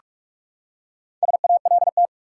Morse code, heard as a keyed tone, about 47 wpm, 700 Hz. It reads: SALT
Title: Four Letter Words - 45wpm